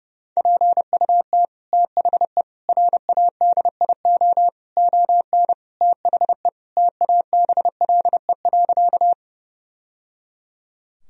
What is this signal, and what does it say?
Morse code, heard as a keyed tone, about 30 wpm, 700 Hz. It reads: PUT THE RADIO ON THE TABLE.
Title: Sentences from Top 1000 Words - Review Part 3 - 30wpm